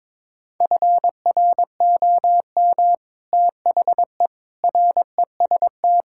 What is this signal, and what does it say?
Morse code, heard as a keyed tone, about 22 wpm, 700 Hz. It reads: FROM THE REST
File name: practice / 22wpm / Build A Phrase - 5-Word Phrases - 22wpm